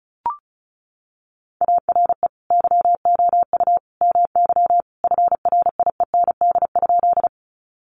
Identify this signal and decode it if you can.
Morse code, heard as a keyed tone, about 34 wpm, 700 Hz. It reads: ARE YOU MY FRIEND?
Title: Sentences from Top 300 Words 35wpm